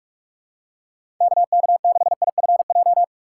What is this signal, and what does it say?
Morse code, heard as a keyed tone, about 41 wpm, 700 Hz. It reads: KK6IFJ
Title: USA Call Signs - 6 Characters - 2x Character Spacing - 45wpm